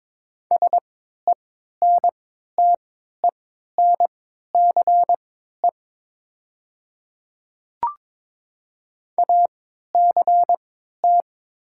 Morse code, heard as a keyed tone, about 22 wpm, spaced as for 13 wpm, 690 Hz. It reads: SENTENCE ACT
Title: 3x Character Spacing - Top 300 Words - 22wpm